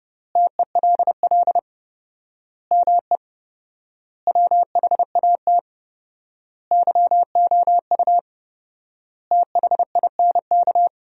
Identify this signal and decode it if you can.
Morse code, heard as a keyed tone, about 30 wpm, 700 Hz. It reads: TELL ME WHAT YOU THINK
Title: 4x Word-Spacing - Sentences from Top 200 Words - 30wpm